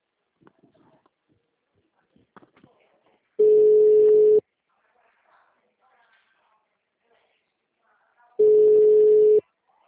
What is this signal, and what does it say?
Ringback tone (Europe, 425 Hz).
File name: out- -2002- - - .918